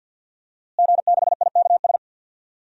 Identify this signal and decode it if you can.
Morse code, heard as a keyed tone, about 49 wpm, 700 Hz. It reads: G6IKS
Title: DXCC-100 Call Signs - 5 Characters - Repeated 2x - 50wpm